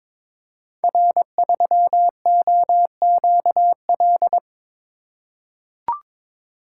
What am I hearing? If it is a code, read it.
Morse code, heard as a keyed tone, about 22 wpm, 700 Hz. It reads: R3OQL
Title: Code Groups - 5 Letter:Numbers - 22wpm